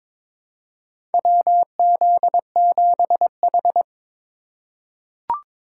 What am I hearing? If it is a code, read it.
Morse code, heard as a keyed tone, about 22 wpm, 700 Hz. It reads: WZ7H
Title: USA Call Signs - 4 Characters - 2x Character Spacing - 22wpm